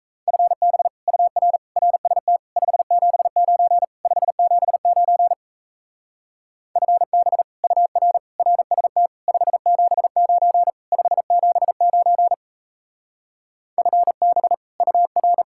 Morse code, heard as a keyed tone, about 35 wpm, 700 Hz. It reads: FB UR RST 579 579 FB UR RST 579 579 FB UR